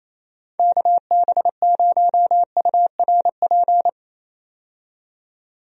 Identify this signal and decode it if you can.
Morse code, heard as a keyed tone, about 28 wpm, 700 Hz. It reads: KB0URP
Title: USA Call Signs - 6 Characters - 28wpm